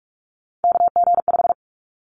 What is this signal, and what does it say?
Morse code, heard as a keyed tone, about 44 wpm, 700 Hz. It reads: KG5
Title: Callsign Prefixes for Top 25 DXCC - 45wpm